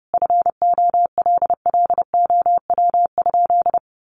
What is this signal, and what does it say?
Morse code, heard as a keyed tone, about 30 wpm, 700 Hz. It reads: FOLLOW?
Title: Sentences from Top 1100 Words 30wpm